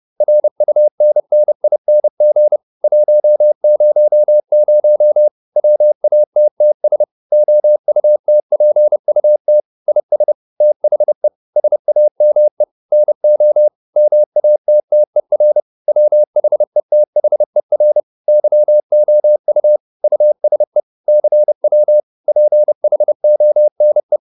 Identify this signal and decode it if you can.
Morse code, heard as a keyed tone, about 30 wpm, 600 Hz. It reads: RUNNING 100 WATTS OUTPUT IS THE SAME NO MATTER WHETHER YOU USE CW PHONE